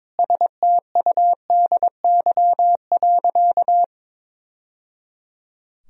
Morse code, heard as a keyed tone, about 22 wpm, 700 Hz. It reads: STUDY.
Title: Sentences from Top 1000 Words - Review Part 3 - 22wpm